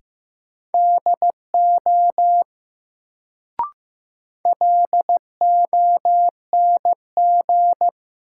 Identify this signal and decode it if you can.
Morse code, heard as a keyed tone, about 15 wpm, 700 Hz. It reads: DO LONG
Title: Sets of 2 Words - New Word Drill - 15wpm